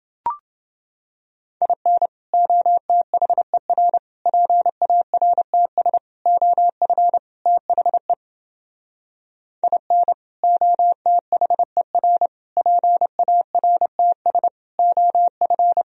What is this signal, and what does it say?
Morse code, heard as a keyed tone, about 31 wpm, 700 Hz. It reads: IN OTHER PARTS OF THE IN OTHER PARTS OF